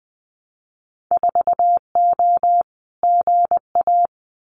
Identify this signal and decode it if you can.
Morse code, heard as a keyed tone, about 20 wpm, 700 Hz. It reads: 4O GA
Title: Winter Field Day Exchanges 20wpm